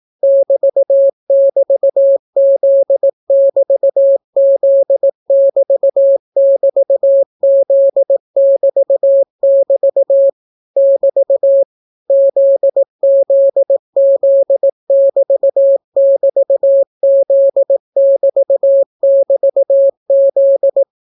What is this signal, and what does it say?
Morse code, heard as a keyed tone, about 18 wpm, 550 Hz. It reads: <BT><BT>Z<BT>Z<BT><BT>Z<BT><BT> <BT> ZZZ<BT><BT>Z<BT><BT>Z